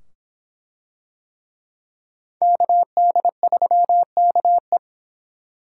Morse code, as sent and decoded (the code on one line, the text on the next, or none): -.- -.. ...-- -.- .
KD3KE